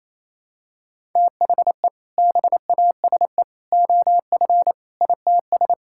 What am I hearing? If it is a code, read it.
Morse code, heard as a keyed tone, about 28 wpm, 700 Hz. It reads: THE BASE OF ITS
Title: Sentences from Top 1000 Words - Review Part 5 - 28wpm